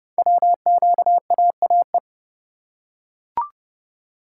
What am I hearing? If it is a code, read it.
Morse code, heard as a keyed tone, about 29 wpm, 700 Hz. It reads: WQAAE
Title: Code Groups - 5 Letter:Numbers - 30wpm